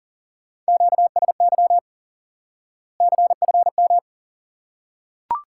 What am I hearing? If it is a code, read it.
Morse code, heard as a keyed tone, about 40 wpm, 700 Hz. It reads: QSY CFM
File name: IQE Course 3 - Lesson 89 - YAGI - Two Elements - 40wpm